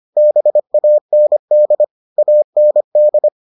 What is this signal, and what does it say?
Morse code, heard as a keyed tone, about 25 wpm, 600 Hz. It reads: BAND AND